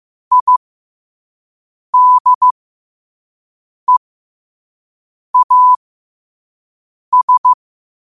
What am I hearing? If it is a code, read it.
Morse code, heard as a keyed tone, about 15 wpm, spaced as for 5 wpm, 1000 Hz. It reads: IDEAS
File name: qotdfiles / 5wpm